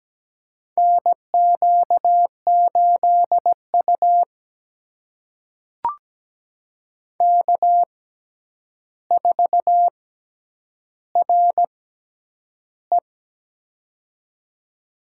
Morse code, heard as a keyed tone, about 17 wpm, spaced as for 9 wpm, 700 Hz. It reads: NQ8U K4RE
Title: USA Call Signs - 4 Characters - 6x Character Spacing - 17wpm